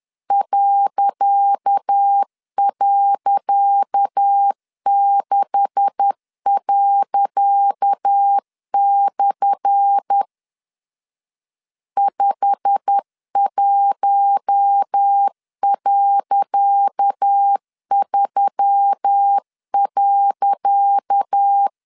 Morse code, heard as a keyed tone, about 10 wpm, 800 Hz. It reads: ..6./ 51.3.
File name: LEZIONI / 10 SIGNS / 10C DOT 10wpm 2940